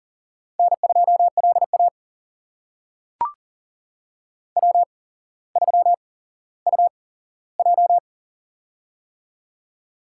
Morse code, heard as a keyed tone, about 39 wpm, spaced as for 17 wpm, 700 Hz. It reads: N2LA W3UJ
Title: USA Call Signs - 4 Characters - 8x Character Spacing - 40wpm